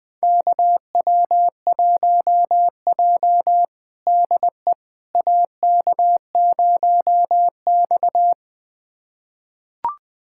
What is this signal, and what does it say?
Morse code, heard as a keyed tone, about 21 wpm, 700 Hz. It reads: KW1J DE AK0X
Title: USA Callsign DE Callsign - 4 Characters - 20wpm